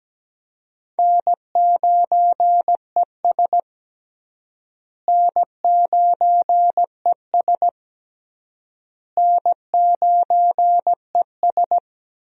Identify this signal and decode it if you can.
Morse code, heard as a keyed tone, about 17 wpm, 700 Hz. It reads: N9ES N9ES N9ES
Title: USA Call Signs - 4 Characters - Repeated 3x - 17wpm